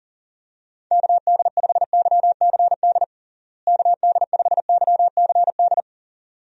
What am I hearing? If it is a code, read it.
Morse code, heard as a keyed tone, about 37 wpm, 700 Hz. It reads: KD5YCD KD5YCD
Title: USA Call Signs - 6 Characters - Repeated 2x - 40wpm